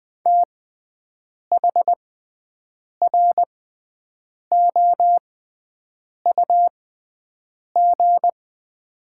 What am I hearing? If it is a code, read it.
Morse code, heard as a keyed tone, about 21 wpm, spaced as for 7 wpm, 700 Hz. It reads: THROUG
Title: 6x Character Spacing - Top 200 Words - 20wpm